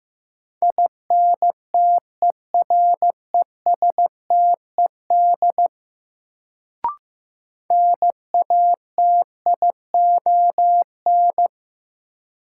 Morse code, heard as a keyed tone, about 15 wpm, 700 Hz. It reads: INTERESTED NATION
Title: Sentences from Top 900 Words - New Word Drill - 15wpm